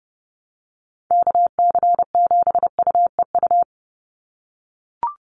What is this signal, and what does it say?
Morse code, heard as a keyed tone, about 29 wpm, 700 Hz. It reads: KC7UEU
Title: USA Call Signs 30wpm